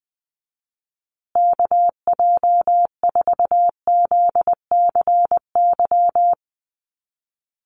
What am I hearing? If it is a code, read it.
Morse code, heard as a keyed tone, about 20 wpm, 700 Hz. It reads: KJ4ZCY
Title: USA Call Signs Repeated 2x 20wpm